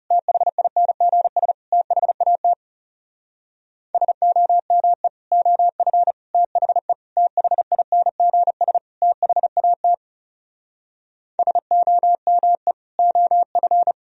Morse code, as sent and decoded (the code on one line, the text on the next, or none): - .... .. -. --. ... / - .... .- - / ... --- -- . / --- ..-. / - .... / - .... .. -. --. ... / - .... .- - / ... --- -- . / --- ..-.
THINGS THAT SOME OF THE THINGS THAT SOME OF